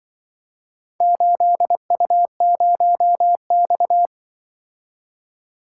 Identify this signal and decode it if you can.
Morse code, heard as a keyed tone, about 25 wpm, 700 Hz. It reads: 8U0X